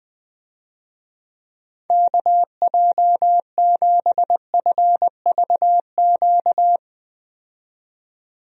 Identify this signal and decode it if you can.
Morse code, heard as a keyed tone, about 20 wpm, 700 Hz. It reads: KJ7FVQ
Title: USA Call Signs - 6 Characters - 20wpm